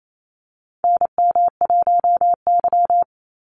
Morse code, heard as a keyed tone, about 28 wpm, 700 Hz. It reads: NM1Y
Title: Call Signs - 28wpm